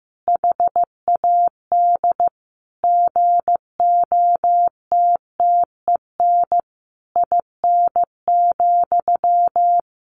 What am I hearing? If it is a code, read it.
Morse code, heard as a keyed tone, about 15 wpm, 700 Hz. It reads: HAD GOTTEN IN,
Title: A Sound Of Thunder 15wpm